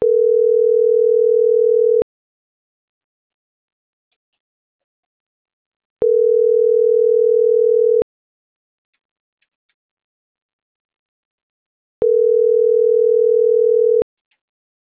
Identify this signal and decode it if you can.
Ringback tone (North America), 440 + 480 Hz.